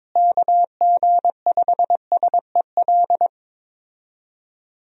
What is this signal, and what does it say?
Morse code, heard as a keyed tone, about 22 wpm, 700 Hz. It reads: KG5SEL
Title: USA Call Signs - 6 Characters - Repeated 3x - 22wpm